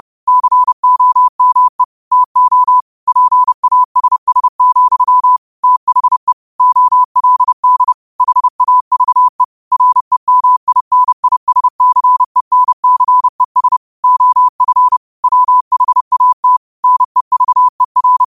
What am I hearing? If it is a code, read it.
Morse code, heard as a keyed tone, about 30 wpm, 1000 Hz. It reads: COME TO PASS, THE OLD HAVE REMINISCENCES OF WHAT NEVER
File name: qotdfiles / 30wpm